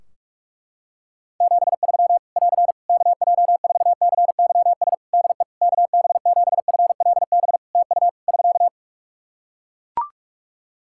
Morse code, heard as a keyed tone, about 42 wpm, 700 Hz. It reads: 73 <AR> KJ4CYS DE KB7FLB TU <SK>
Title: Practice QSO - Exchange 4 - Ending - 45wpm